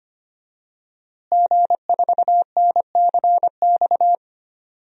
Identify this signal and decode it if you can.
Morse code, heard as a keyed tone, about 25 wpm, 700 Hz. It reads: G4NCX
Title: DXCC-100 Call Signs - 5 Characters - Repeated 2x - 25wpm